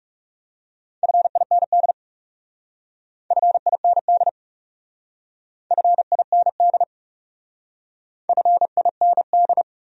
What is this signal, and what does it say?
Morse code, heard as a keyed tone, about 38 wpm, 700 Hz. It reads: FIND FIND FIND FIND